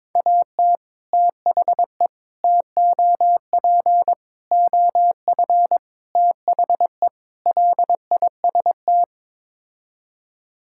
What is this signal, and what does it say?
Morse code, heard as a keyed tone, about 21 wpm, 700 Hz. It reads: AT THE TOP OF THE LIST
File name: practice / 22wpm / Sets of 6 Words - 22wpm